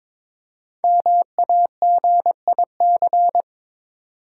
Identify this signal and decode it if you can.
Morse code, heard as a keyed tone, about 22 wpm, 700 Hz. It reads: MAGIC
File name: 5 Letter Words - First 3 Letters Given - 22wpm